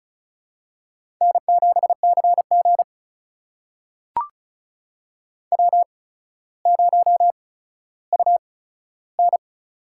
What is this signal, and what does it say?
Morse code, heard as a keyed tone, about 36 wpm, spaced as for 15 wpm, 700 Hz. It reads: N7CG W0UN
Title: USA Call Signs - 4 Characters - 8x Character Spacing - 35wpm